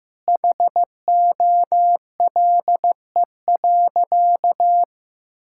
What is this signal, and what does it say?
Morse code, heard as a keyed tone, about 16 wpm, 700 Hz. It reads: HOLE.
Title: Sentences from Top 1600 Words 15wpm